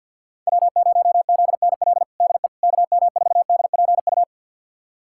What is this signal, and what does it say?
Morse code, heard as a keyed tone, about 53 wpm, 700 Hz. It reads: W0ZNL DE KM4DPU
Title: USA Callsign DE Callsign 50wpm